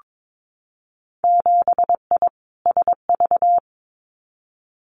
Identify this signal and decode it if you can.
Morse code, heard as a keyed tone, about 22 wpm, 700 Hz. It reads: 7I SV
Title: Winter Field Day Exchanges Encore 22wpm